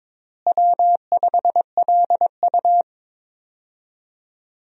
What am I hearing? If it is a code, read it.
Morse code, heard as a keyed tone, about 21 wpm, 700 Hz. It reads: W5LU
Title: USA Call Signs - 4 Characters - Repeated 3x - 22wpm